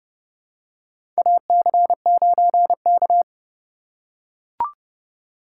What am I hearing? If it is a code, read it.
Morse code, heard as a keyed tone, about 30 wpm, 700 Hz. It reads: AC9K
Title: USA Call Signs - 4 Characters - 3x Character Spacing - 30wpm